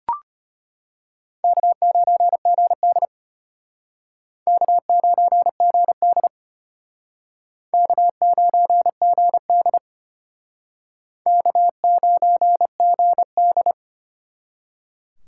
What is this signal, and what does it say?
Morse code, heard as a keyed tone, about 32 wpm, 700 Hz. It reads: K9GD K9GD K9GD K9GD